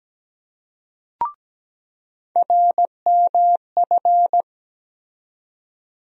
Morse code, heard as a keyed tone, about 17 wpm, 700 Hz. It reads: RMF
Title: USA Call Sign Postfixes - 3 Letters 17wpm